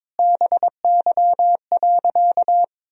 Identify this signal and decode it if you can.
Morse code, heard as a keyed tone, about 22 wpm, 700 Hz. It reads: BY.
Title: Sentences from Top 1000 Words - Review Part 1 - 22wpm